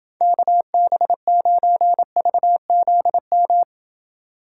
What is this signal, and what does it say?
Morse code, heard as a keyed tone, about 28 wpm, 700 Hz. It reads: KB9VZM